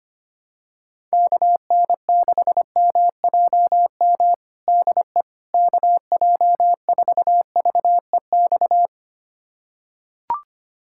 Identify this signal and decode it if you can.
Morse code, heard as a keyed tone, about 25 wpm, 700 Hz. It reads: KN6MJM DE KJ4VEX
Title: USA Callsign DE Callsign - 6 Characters - 25wpm